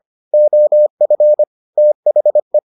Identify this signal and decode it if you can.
Morse code, heard as a keyed tone, about 25 wpm, 600 Hz. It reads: OF THE